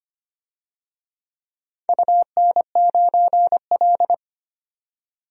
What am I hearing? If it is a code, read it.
Morse code, heard as a keyed tone, about 25 wpm, 700 Hz. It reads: UN9L
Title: DXCC-100 Call Signs - 4 Characters - 25wpm